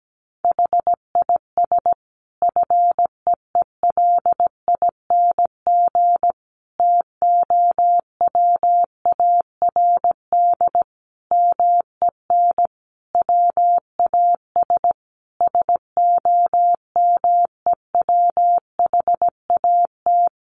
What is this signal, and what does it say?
Morse code, heard as a keyed tone, about 17 wpm, 700 Hz. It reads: HIS FEELING TOWARD MEN WAS SOMEWHAT
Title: Sentences from Top 1300 Words 17wpm